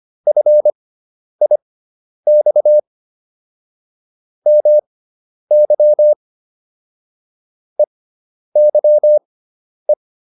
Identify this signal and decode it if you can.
Morse code, heard as a keyed tone, about 25 wpm, spaced as for 10 wpm, 600 Hz. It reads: FIX MY EYE